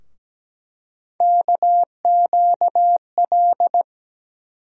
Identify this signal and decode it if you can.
Morse code, heard as a keyed tone, about 17 wpm, 700 Hz. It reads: KQL
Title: USA Call Sign Postfixes - 3 Letters 17wpm